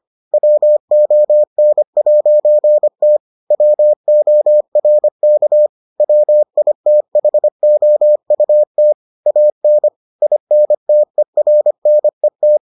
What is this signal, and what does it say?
Morse code, heard as a keyed tone, about 25 wpm, 600 Hz. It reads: WON'T WORK WITHOUT AN INTERNET